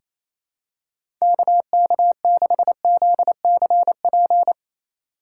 Morse code, heard as a keyed tone, about 28 wpm, 700 Hz. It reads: KK6ZCP
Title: USA Call Signs - 6 Characters - 3x Character Spacing - 28wpm